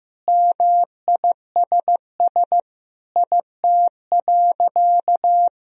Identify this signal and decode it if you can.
Morse code, heard as a keyed tone, about 15 wpm, 700 Hz. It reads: MISS IT.
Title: Sentences from Top 1000 Words - Review Part 3 - 15wpm